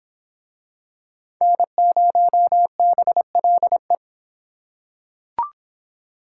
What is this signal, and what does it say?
Morse code, heard as a keyed tone, about 22 wpm, 700 Hz. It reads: N0BLE